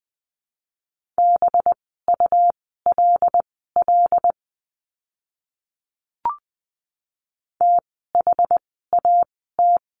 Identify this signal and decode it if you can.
Morse code, heard as a keyed tone, about 20 wpm, spaced as for 15 wpm, 700 Hz. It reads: BULL THAT
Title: 185 Lesson 30f - Letter b - Words 20wpm